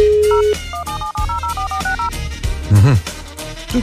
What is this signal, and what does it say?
DTMF, heard as telephone touch tones, tones of about 120 ms. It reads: 0 17704460